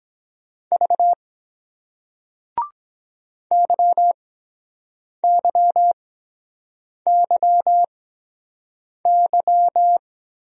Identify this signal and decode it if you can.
Morse code, heard as a keyed tone, about 23 wpm, spaced as for 7 wpm, 700 Hz. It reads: V YYYY